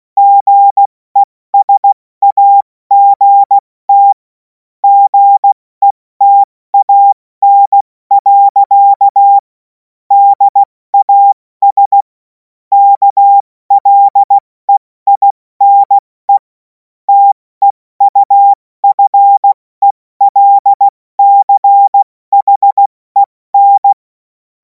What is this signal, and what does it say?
Morse code, heard as a keyed tone, about 16 wpm, 800 Hz. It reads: GESAGT GETAN. DAS KLEINE TEUFELCHEN
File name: Morsekurs / Witze / Witz 03. März 2026 70 F80